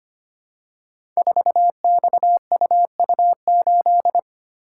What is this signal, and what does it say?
Morse code, heard as a keyed tone, about 25 wpm, 700 Hz. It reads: 4XUU8
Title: Code Groups - 5 Letter:Numbers - 25wpm